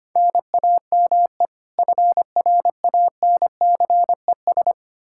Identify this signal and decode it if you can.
Morse code, heard as a keyed tone, about 25 wpm, 700 Hz. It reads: NAME FRANCES
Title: Practice QSO - Exchange 1 - Beginning - 25wpm